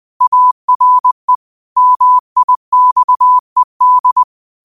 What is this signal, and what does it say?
Morse code, heard as a keyed tone, about 20 wpm, 1000 Hz. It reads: ARE MIXED